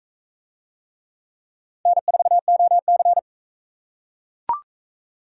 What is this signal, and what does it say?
Morse code, heard as a keyed tone, about 42 wpm, 700 Hz. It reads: N4OC